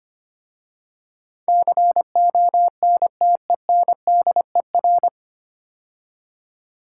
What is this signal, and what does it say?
Morse code, heard as a keyed tone, about 25 wpm, 700 Hz. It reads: CONTENDER
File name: practice / 25wpm / 9 Letter Words - First 3 Letters Given - 25wpm